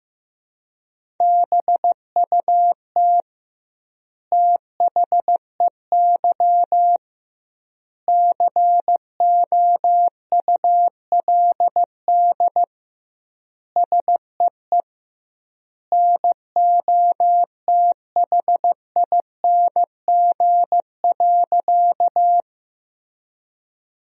Morse code, heard as a keyed tone, about 15 wpm, 700 Hz. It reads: BUT THEY COULD SEE NOTHING.
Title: 2x Word-Spacing - Sentences from Top 500 Words - 15wpm